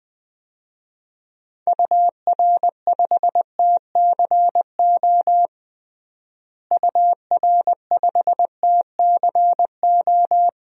Morse code, heard as a keyed tone, about 20 wpm, 700 Hz. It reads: UR5TCO UR5TCO
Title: DXCC-100 Call Signs - 6 Characters - Repeated 2x - 20wpm